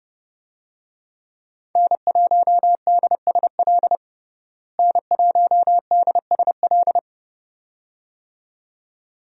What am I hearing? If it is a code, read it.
Morse code, heard as a keyed tone, about 30 wpm, 700 Hz. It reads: N1DSL N1DSL
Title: USA Call Signs - 5 Characters - Repeated 2x - 30wpm